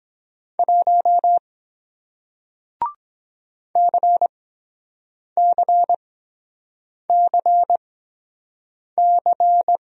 Morse code, heard as a keyed tone, about 24 wpm, spaced as for 7 wpm, 700 Hz. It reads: 1 CCCC